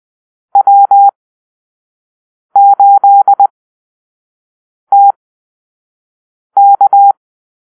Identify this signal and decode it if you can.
Morse code, heard as a keyed tone, about 20 wpm, spaced as for 5 wpm, 800 Hz. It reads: W8TK
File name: arrl / Call-Sign Drill-1